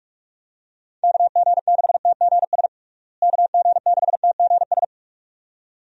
Morse code, heard as a keyed tone, about 42 wpm, 700 Hz. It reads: KG6TGS KG6TGS
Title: USA Call Signs - 6 Characters - Repeated 2x - 45wpm